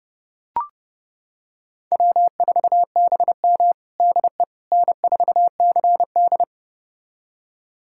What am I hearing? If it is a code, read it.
Morse code, heard as a keyed tone, about 32 wpm, 700 Hz. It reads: W4BM DE N4CD
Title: USA Callsign DE Callsign - 4 Characters - 30wpm